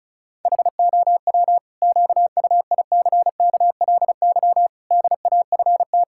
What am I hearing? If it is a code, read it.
Morse code, heard as a keyed tone, about 34 wpm, 700 Hz. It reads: HOW QUICKLY DAFT
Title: Pangram Challenge v2 - 35wpm